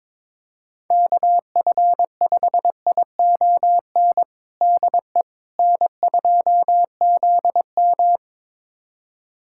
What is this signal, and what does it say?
Morse code, heard as a keyed tone, about 21 wpm, 700 Hz. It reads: KF5ION DE N2ZM
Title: USA Callsign DE Callsign 22wpm